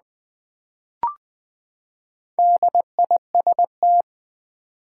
Morse code, heard as a keyed tone, about 20 wpm, 700 Hz. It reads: DIST
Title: Build A Word - 4 Letters At A Time - 12-Letter Words - 20wpm